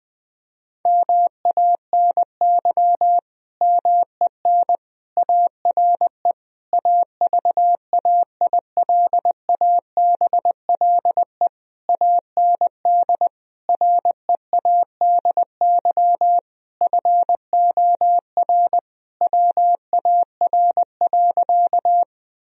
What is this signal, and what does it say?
Morse code, heard as a keyed tone, about 20 wpm, 700 Hz. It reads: MANY MEN ARE AVAILABLE AND READY FOR WAR.
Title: Sentences from Top 1000 Words - Review Part 3 - 20wpm